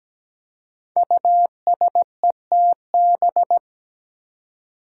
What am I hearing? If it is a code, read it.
Morse code, heard as a keyed tone, about 17 wpm, 700 Hz. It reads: USETB
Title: Five Challenging Letter-Number Combinations 17wpm